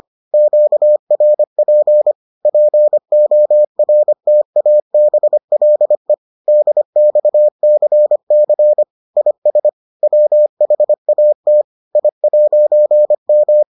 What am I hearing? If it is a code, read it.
Morse code, heard as a keyed tone, about 25 wpm, 600 Hz. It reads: QRP PORTABLE DXCC IS WHAT I'M